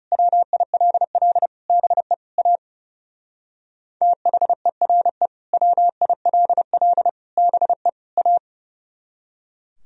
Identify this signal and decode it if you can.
Morse code, heard as a keyed tone, about 37 wpm, 700 Hz. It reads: WILL BE A THERE WILL BE A
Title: Speed-Racing - Sets of 4 Words 30wpm